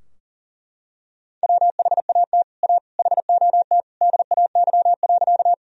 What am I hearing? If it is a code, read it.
Morse code, heard as a keyed tone, about 37 wpm, 700 Hz. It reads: WHAT A HOT DAY.
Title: Sentences from Top 1000 Words - Review Part 3 - 40wpm